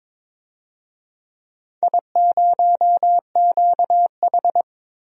Morse code, heard as a keyed tone, about 22 wpm, 700 Hz. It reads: I0QH
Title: DXCC-100 Call Signs - 4 Characters - Repeated 3x - 22wpm